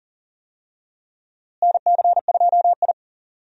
Morse code, heard as a keyed tone, about 40 wpm, 700 Hz. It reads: NC2I
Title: USA Call Signs - 4 Characters - Repeated 2x - 40wpm